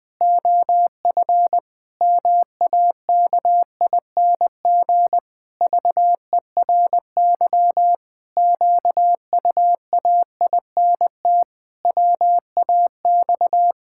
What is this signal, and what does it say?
Morse code, heard as a keyed tone, about 19 wpm, 700 Hz. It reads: OF MAKING VERY QUAINT WAX
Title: Pangram Challenge v2 - 20wpm